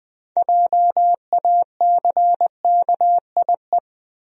Morse code, heard as a keyed tone, about 20 wpm, 700 Hz. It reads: JACKIE
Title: USA Common Names - 20wpm